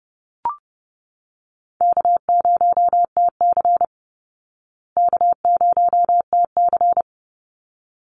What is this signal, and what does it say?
Morse code, heard as a keyed tone, about 30 wpm, 700 Hz. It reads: K0TC K0TC
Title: Easy Call Signs Repeated 2x - 30wpm